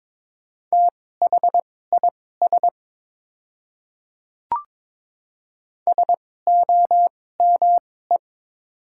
Morse code, heard as a keyed tone, about 22 wpm, spaced as for 16 wpm, 700 Hz. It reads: THIS SOME
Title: 2x Character Spacing - Top 100 Words - 22wpm